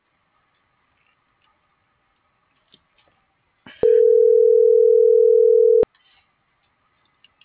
Ringback tone (North America), 440 + 480 Hz.